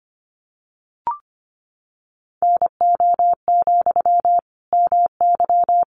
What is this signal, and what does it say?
Morse code, heard as a keyed tone, about 25 wpm, 700 Hz. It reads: NO, MY
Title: Sentences from Top 400 Words 25wpm